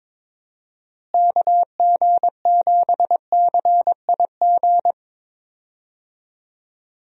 Morse code, heard as a keyed tone, about 22 wpm, 700 Hz. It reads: KG7CIG
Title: USA Call Signs - 6 Characters - Repeated 2x - 22wpm